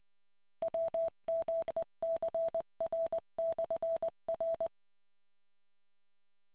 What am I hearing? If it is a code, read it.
Morse code, heard as a keyed tone, about 20 wpm, 660 Hz. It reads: WMECR/R